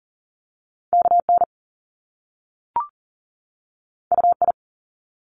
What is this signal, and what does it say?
Morse code, heard as a keyed tone, about 38 wpm, 700 Hz. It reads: KN UI
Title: Top Two Letter Combinations in Words 40wpm